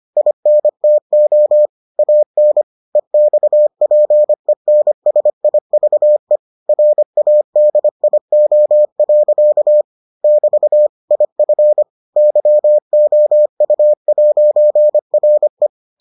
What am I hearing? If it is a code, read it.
Morse code, heard as a keyed tone, about 25 wpm, 600 Hz. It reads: INTO AN EXPENSIVE RADIO. <BT> IF YOU'RE